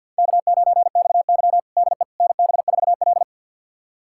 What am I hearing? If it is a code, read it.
Morse code, heard as a keyed tone, about 48 wpm, 700 Hz. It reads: K9XY DE NB4L
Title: USA Callsign DE Callsign - 4 Characters - 50wpm